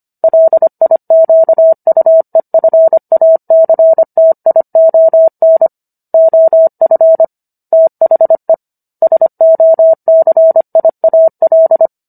Morse code, heard as a keyed tone, about 25 wpm, 650 Hz. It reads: LIQUEFACTION OF THE SOCIAL
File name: Book One / 16-Chapter 16 - 25 WPM